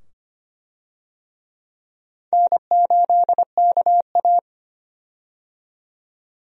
Morse code, heard as a keyed tone, about 25 wpm, 700 Hz. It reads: N8KA
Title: USA Call Signs - 4 Characters - 25wpm